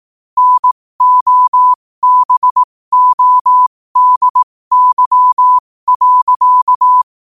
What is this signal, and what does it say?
Morse code, heard as a keyed tone, about 18 wpm, spaced as for 15 wpm, 1000 Hz. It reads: NOBODY.
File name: qotdfiles / 15wpm